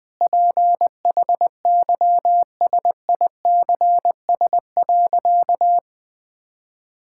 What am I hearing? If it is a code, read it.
Morse code, heard as a keyed tone, about 21 wpm, 700 Hz. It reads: PHYSICS.
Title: Sentences from Top 1000 Words - Review Part 5 - 20wpm